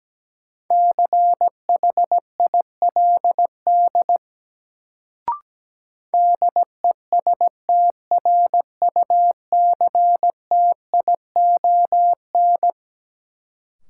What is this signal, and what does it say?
Morse code, heard as a keyed tone, about 17 wpm, 700 Hz. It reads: CHILD DESTRUCTION
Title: Sets of 5 Words Encore - New Word Drill - 17wpm